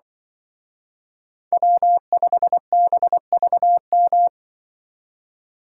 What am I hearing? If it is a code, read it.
Morse code, heard as a keyed tone, about 25 wpm, 700 Hz. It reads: W5BVM
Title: Speed Racing - USA 1x3 Call Signs 20wpm